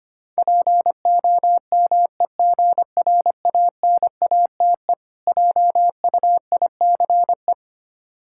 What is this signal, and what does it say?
Morse code, heard as a keyed tone, about 25 wpm, 700 Hz. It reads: POMEGRANATE JUICE
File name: practice / 25wpm / Difficult Words with Hints - 25wpm